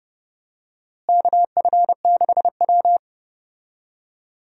Morse code, heard as a keyed tone, about 30 wpm, 700 Hz. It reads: KF6W
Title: USA Call Signs - 4 Characters - Repeated 2x - 30wpm